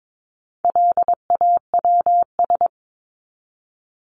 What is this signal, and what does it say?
Morse code, heard as a keyed tone, about 22 wpm, 700 Hz. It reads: LAWS
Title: ES:S Suffix - Top 100 Words with Base Words 22wpm